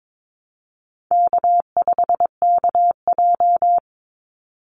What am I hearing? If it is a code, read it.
Morse code, heard as a keyed tone, about 22 wpm, 700 Hz. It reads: K5KJ
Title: Call Signs Repeated 2x - 22wpm